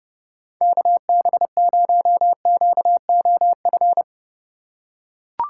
Morse code, heard as a keyed tone, about 30 wpm, 700 Hz. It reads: KB0QOF
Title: USA Call Signs - 6 Characters - 3x Character Spacing - 30wpm